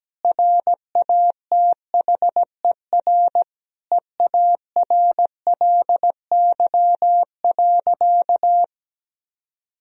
Morse code, heard as a keyed tone, about 17 wpm, 700 Hz. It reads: RATHER EARLY.
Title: Sentences from Top 1000 Words - Review Part 4 - 17wpm